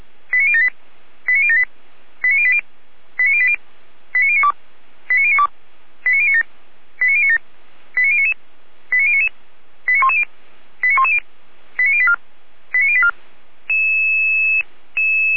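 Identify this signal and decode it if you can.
Five-tone selective call (ZVEI), heard as quick runs of five tones, tones of about 70 ms. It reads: 89078 89078 89080 89080 89092 89092 89097 89097 89099 89099 89110 89110 89084 89084
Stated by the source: SDR-received ZVEI five-tone selective call, address 89072.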